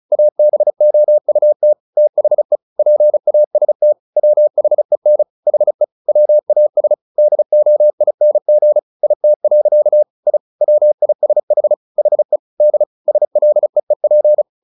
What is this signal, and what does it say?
Morse code, heard as a keyed tone, about 34 wpm, 600 Hz. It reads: ABOUT THE PAST WHEN HE WAS DOING IT. I WISH HE D SLEEP
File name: the old man and the sea / the old man and the sea 1 35wpm